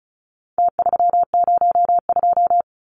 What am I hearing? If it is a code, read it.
Morse code, heard as a keyed tone, about 34 wpm, 700 Hz. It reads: T302
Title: Numbers - Contest Style - 4-Digit 35wpm